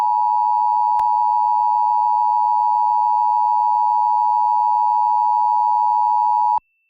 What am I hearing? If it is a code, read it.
Emergency Alert System (SAME) tones (the 853 and 960 Hz attention tone). It is heard from 1.0 to 6.6 s.
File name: wrct-log hi- 08